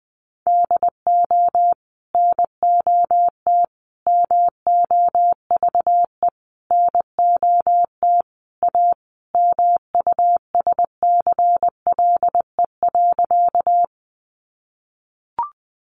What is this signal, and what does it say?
Morse code, heard as a keyed tone, about 20 wpm, 700 Hz. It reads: DO NOT MOVE NOT A MUSCLE.
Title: Sentences from Top 1300 Words 20wpm